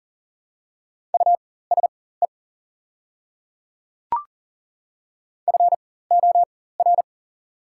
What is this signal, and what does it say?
Morse code, heard as a keyed tone, about 40 wpm, spaced as for 19 wpm, 690 Hz. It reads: USE FOR
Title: 4x Character Spacing - Top 100 Words - 40wpm